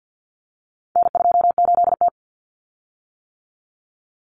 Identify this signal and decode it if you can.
Morse code, heard as a keyed tone, about 50 wpm, 700 Hz. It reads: N28T
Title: Common Cut Numbers - 4-Digit 50wpm